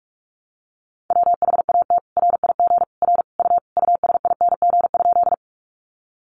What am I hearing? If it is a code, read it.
Morse code, heard as a keyed tone, about 44 wpm, 700 Hz. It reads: WHAT RIG R U USING?
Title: Common QSO Phrases 45wpm